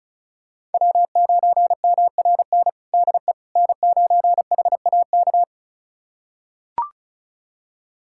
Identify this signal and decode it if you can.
Morse code, heard as a keyed tone, about 36 wpm, 700 Hz. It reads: W9MRN DE N9HAK
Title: USA Callsign DE Callsign - 5 Characters - 35wpm